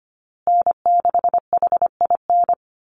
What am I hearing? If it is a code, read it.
Morse code, heard as a keyed tone, about 25 wpm, 700 Hz. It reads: N6HIN
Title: USA Call Signs Repeated 2x 25wpm